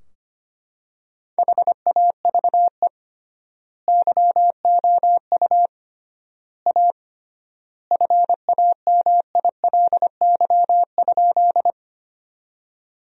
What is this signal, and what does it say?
Morse code, heard as a keyed tone, about 25 wpm, 700 Hz. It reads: HAVE YOU A FAMILY?
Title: 3x Word-Spacing - Sentences from Top 400 Words - 25wpm